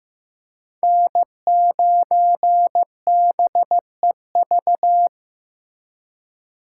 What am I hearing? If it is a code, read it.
Morse code, heard as a keyed tone, about 18 wpm, 700 Hz. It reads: N9BEV
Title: Speed Racing - USA 1x3 Call Signs 15wpm